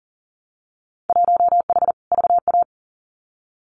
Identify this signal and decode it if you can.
Morse code, heard as a keyed tone, about 37 wpm, 700 Hz. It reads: 1H VA
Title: Winter Field Day Exchanges 40wpm